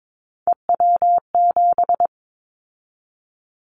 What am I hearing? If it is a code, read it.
Morse code, heard as a keyed tone, about 22 wpm, 700 Hz. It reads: EW7
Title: Callsign Prefixes for Top 50 DXCC - 22wpm